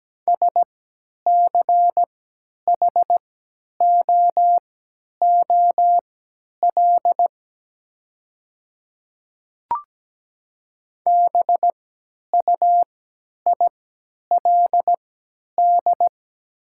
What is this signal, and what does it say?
Morse code, heard as a keyed tone, about 17 wpm, spaced as for 10 wpm, 700 Hz. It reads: SCHOOL BUILD
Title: 3x Character Spacing - Top 300 Words - 17wpm